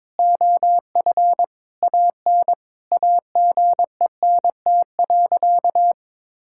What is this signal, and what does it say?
Morse code, heard as a keyed tone, about 22 wpm, 700 Hz. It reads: OF AN AGENT.
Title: Sentences from Top 1600 Words 22wpm